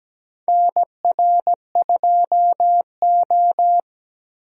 Morse code, heard as a keyed tone, about 17 wpm, 700 Hz. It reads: NR2O
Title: USA Call Signs - 4 Characters - 17wpm